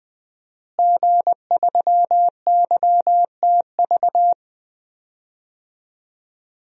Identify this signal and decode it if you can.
Morse code, heard as a keyed tone, about 20 wpm, 700 Hz. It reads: G3YTV
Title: DXCC-100 Call Signs - 5 Characters - Repeated 3x - 20wpm